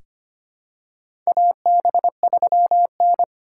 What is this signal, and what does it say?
Morse code, heard as a keyed tone, about 25 wpm, 700 Hz. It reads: AB3N
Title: USA Call Signs - 6x Character Spacing - 25wpm